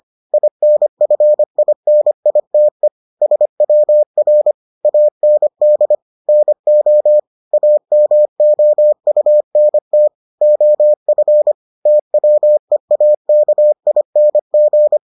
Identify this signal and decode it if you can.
Morse code, heard as a keyed tone, about 25 wpm, 600 Hz. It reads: INFINITE SWR AND NO AMOUNT OF TWEAKING